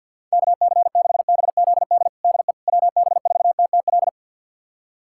Morse code, heard as a keyed tone, about 51 wpm, 700 Hz. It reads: KC6BZD DE WB4TTL
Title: USA Callsign DE Callsign - 6 Characters - 50wpm